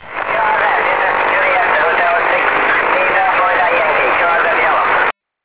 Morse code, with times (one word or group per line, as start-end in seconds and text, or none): none